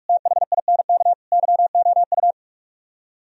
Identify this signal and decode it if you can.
Morse code, heard as a keyed tone, about 59 wpm, 700 Hz. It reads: THINK YOU